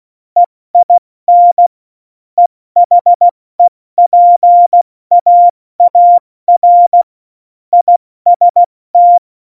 Morse code, heard as a keyed tone, about 16 wpm, 700 Hz. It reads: EIN EHEPAAR IST
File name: Morsekurs / Witze / Witz 27. Oktober 2020, 70 F80